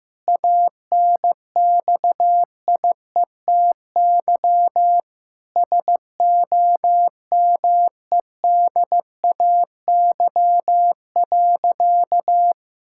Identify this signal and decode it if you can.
Morse code, heard as a keyed tone, about 16 wpm, 700 Hz. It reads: ANXIETY SOMEDAY.
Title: Sentences from Top 1600 Words 15wpm